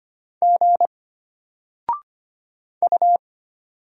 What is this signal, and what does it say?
Morse code, heard as a keyed tone, about 25 wpm, 700 Hz. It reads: G U
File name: Single-Letter-Number-25wpm